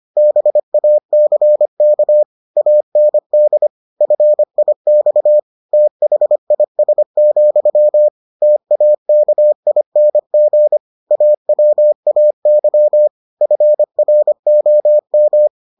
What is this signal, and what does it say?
Morse code, heard as a keyed tone, about 25 wpm, 600 Hz. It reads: BACK AND FIX THIS, TAKING AWAY FROM